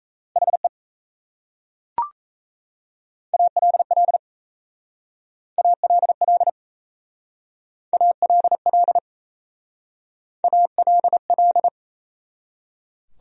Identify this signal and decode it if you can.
Morse code, heard as a keyed tone, about 34 wpm, 700 Hz. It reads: HE ALL ALL ALL ALL
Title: Speed-Racing Top 100 Words 28wpm